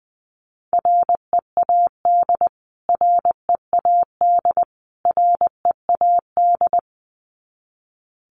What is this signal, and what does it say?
Morse code, heard as a keyed tone, about 20 wpm, 700 Hz. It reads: READ READ READ
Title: Top 200 Words Repeated 20wpm